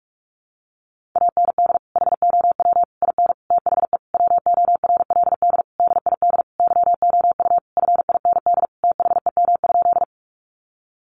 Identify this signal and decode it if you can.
Morse code, heard as a keyed tone, about 44 wpm, 700 Hz. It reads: AND HOW IN THE WORLD DID YOU FIND THEM?
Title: Sentences from Top 200 Words 45wpm